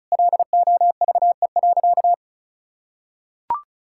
Morse code, heard as a keyed tone, about 34 wpm, 700 Hz. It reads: LOVE.
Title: Sentences from Top 1600 Words 35wpm